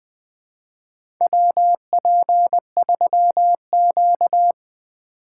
Morse code, heard as a keyed tone, about 20 wpm, 700 Hz. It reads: WP3Q
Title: USA Call Signs - 4 Characters - 3x Character Spacing - 20wpm